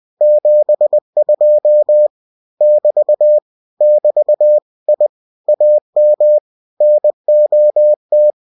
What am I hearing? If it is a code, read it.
Morse code, heard as a keyed tone, about 20 wpm, 600 Hz. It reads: 72 <BT> <BT> I AM NOT
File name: cw-books / winnie-the-pooh / winnie-20wpm-page-0072